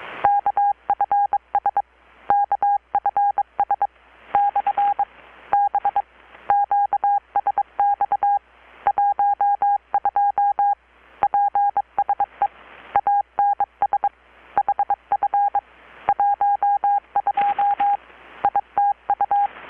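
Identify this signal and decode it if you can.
Morse code, heard as a keyed tone, about 23 wpm, spaced as for 20 wpm, 810 Hz. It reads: KFS KFS/B QSX 12 PSE ANS HF 12 ITU
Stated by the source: Morse from KPH's Night of Nights XX maritime radiotelegraphy event, received in Chiapas, Mexico about 3,800 km away on a Kenwood R-600 with a grounded 10 m long-wire antenna.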